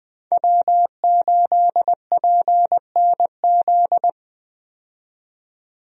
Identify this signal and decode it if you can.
Morse code, heard as a keyed tone, about 20 wpm, 700 Hz. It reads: W8PNZ
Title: USA Call Signs - 5 Characters - 20wpm